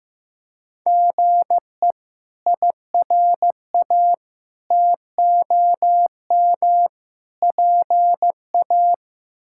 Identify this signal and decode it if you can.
Morse code, heard as a keyed tone, about 15 wpm, 700 Hz. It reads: GE IRA TOM PA
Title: K1USN SST QSO Practice - 15wpm